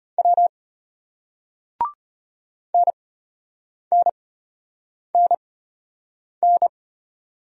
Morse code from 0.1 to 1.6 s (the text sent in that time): W